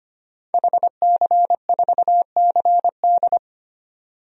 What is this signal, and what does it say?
Morse code, heard as a keyed tone, about 25 wpm, 700 Hz. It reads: HC4CD
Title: DXCC-100 Call Signs - 5 Characters - 25wpm